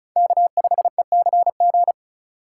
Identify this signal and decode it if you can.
Morse code, heard as a keyed tone, about 34 wpm, 700 Hz. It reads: K5ECG
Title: USA Call Signs - 5 Characters - 3x Character Spacing - 35wpm